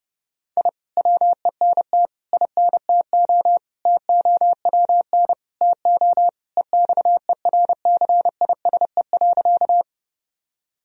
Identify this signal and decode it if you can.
Morse code, heard as a keyed tone, about 30 wpm, 700 Hz. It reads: I WENT INTO TOWN TO EXERCISE.
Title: Sentences from Top 1000 Words - Review Part 4 - 30wpm